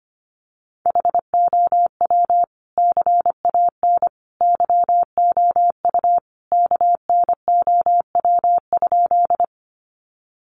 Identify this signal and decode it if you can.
Morse code, heard as a keyed tone, about 25 wpm, 700 Hz. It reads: HOW CAN YOU KNOW?
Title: Sentences from Top 100 Words 25wpm